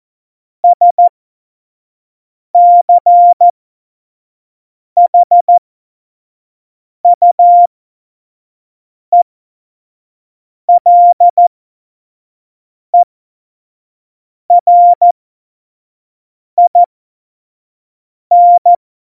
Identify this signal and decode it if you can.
Morse code, heard as a keyed tone, about 14 wpm, spaced as for 5 wpm, 700 Hz. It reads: SCHUELERIN